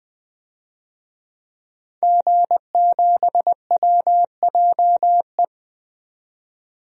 Morse code, heard as a keyed tone, about 20 wpm, 700 Hz. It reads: G7WJE
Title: DXCC-100 Call Signs - 5 Characters - 20wpm